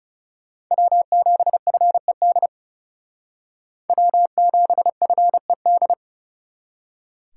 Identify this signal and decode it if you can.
Morse code, heard as a keyed tone, about 36 wpm, 700 Hz. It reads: W7FED W7FED
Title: Speed Racing - USA 1x3 Call Signs 30wpm